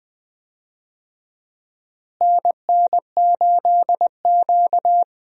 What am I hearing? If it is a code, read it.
Morse code, heard as a keyed tone, about 20 wpm, 700 Hz. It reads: NN8Q
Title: USA Call Signs - 4 Characters - Repeated 2x - 20wpm